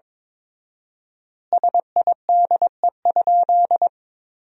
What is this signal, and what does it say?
Morse code, heard as a keyed tone, about 21 wpm, 690 Hz. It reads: SIDE?
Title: 4x Word-Spacing - Sentences from Top 400 Words - 22wpm